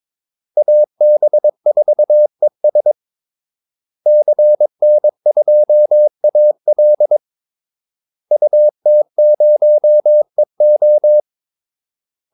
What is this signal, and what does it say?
Morse code, heard as a keyed tone, about 22 wpm, 600 Hz. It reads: AB4ES CN2AL UT0EO
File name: Callsigns-22wpm